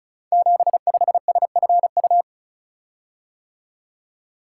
Morse code, heard as a keyed tone, about 36 wpm, 700 Hz. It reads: 75SFU
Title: Code Groups - 5 Letter:Numbers - 35wpm